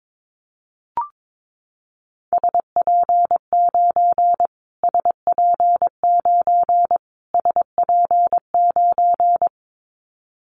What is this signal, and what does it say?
Morse code, heard as a keyed tone, about 22 wpm, 700 Hz. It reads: SP9 SP9 SP9
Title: Callsign Prefixes for Top 25 DXCC - Repeated 3x - 22wpm